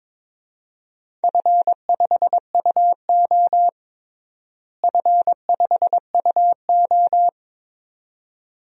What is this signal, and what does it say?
Morse code, heard as a keyed tone, about 22 wpm, 700 Hz. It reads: F5UO F5UO
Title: DXCC-100 Call Signs - 4 Characters - Repeated 2x - 22wpm